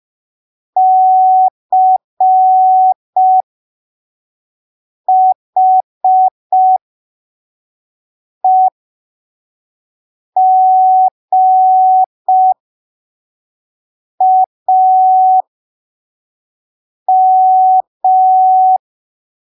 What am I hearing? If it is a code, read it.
Morse code, heard as a keyed tone, about 5 wpm, spaced as for 3 wpm, 750 Hz. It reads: CHEGAM
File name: arquivos / texto / texto 13 05ppm